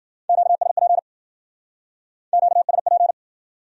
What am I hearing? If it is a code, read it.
Morse code, heard as a keyed tone, about 60 wpm, 700 Hz. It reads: QSP QSP